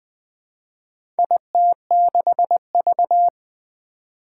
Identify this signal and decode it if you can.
Morse code, heard as a keyed tone, about 20 wpm, 700 Hz. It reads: IT6V
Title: Four Challenging Letter-Number Combinations 20wpm